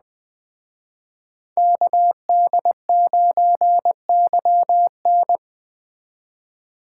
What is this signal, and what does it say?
Morse code, heard as a keyed tone, about 21 wpm, 700 Hz. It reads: KD9YN